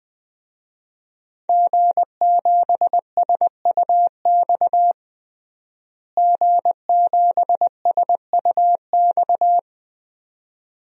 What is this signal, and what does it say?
Morse code, heard as a keyed tone, about 21 wpm, 700 Hz. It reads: G7SUX G7SUX
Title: DXCC-100 Call Signs - 5 Characters - Repeated 2x - 20wpm